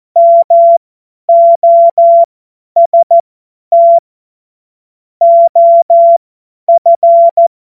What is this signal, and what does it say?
Morse code, heard as a keyed tone, about 14 wpm, spaced as for 10 wpm, 680 Hz. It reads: MOST OF
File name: Morsekurs / Witze / Witz 14. Dezember 2021, 50 F70